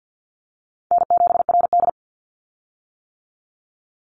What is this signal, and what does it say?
Morse code, heard as a keyed tone, about 50 wpm, 700 Hz. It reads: N7RD
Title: Call Signs - 50wpm